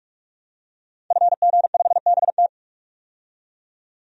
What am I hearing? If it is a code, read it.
Morse code, heard as a keyed tone, about 43 wpm, 700 Hz. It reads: FG5BT
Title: DXCC-100 Call Signs - 5 Characters - 45wpm